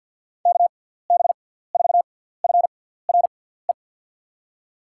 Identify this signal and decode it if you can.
Morse code, heard as a keyed tone, about 52 wpm, spaced as for 17 wpm, 700 Hz. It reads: KB4FRE
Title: USA Call Signs - 6x Character Spacing - 50wpm